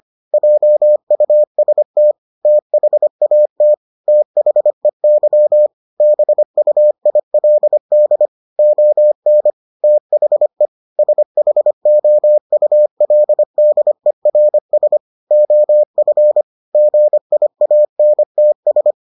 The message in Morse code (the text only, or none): JUST THAT THEY BUILD ON THE SHOULDERS OF GIANTS